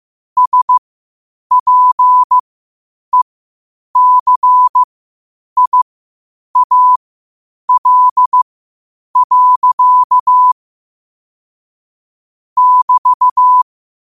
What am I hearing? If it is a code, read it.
Morse code, heard as a keyed tone, about 15 wpm, spaced as for 8 wpm, 1000 Hz. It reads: SPECIAL. <BT>
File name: qotdfiles / 8wpm